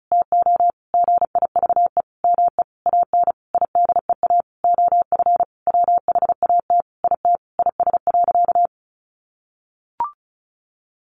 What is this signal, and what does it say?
Morse code, heard as a keyed tone, about 34 wpm, 700 Hz. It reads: TO GIVE ME AN IDEA OF WHAT IT IS.
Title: Sentences from Top 700 Words 35wpm